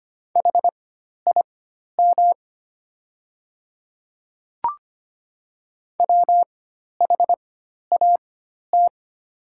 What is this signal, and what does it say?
Morse code, heard as a keyed tone, about 25 wpm, spaced as for 12 wpm, 700 Hz. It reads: HIM WHAT
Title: 4x Character Spacing - Top 100 Words - 25wpm